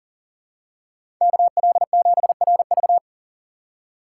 Morse code, heard as a keyed tone, about 40 wpm, 700 Hz. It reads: KL7RV
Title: USA Call Signs - 5 Characters - 4x Character Spacing - 40wpm